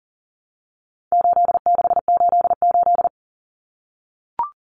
Morse code, heard as a keyed tone, about 38 wpm, 700 Hz. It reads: 8688
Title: Numbers - 4-Digit 40wpm